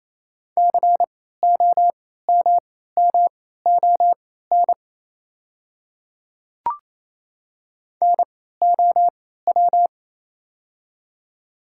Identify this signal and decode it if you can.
Morse code, heard as a keyed tone, about 28 wpm, spaced as for 16 wpm, 690 Hz. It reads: COMMON NOW
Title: 3x Character Spacing - Top 500 Words - 28wpm